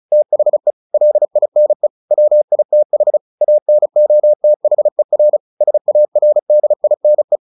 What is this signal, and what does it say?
Morse code, heard as a keyed tone, about 34 wpm, 600 Hz. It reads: THE LINE WITH ANOTHER SARDINE
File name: the old man and the sea 2 35wpm